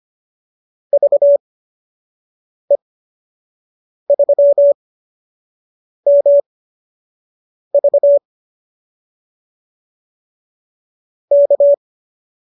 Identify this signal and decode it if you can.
Morse code, heard as a keyed tone, about 25 wpm, spaced as for 6 wpm, 590 Hz. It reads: VE3MV K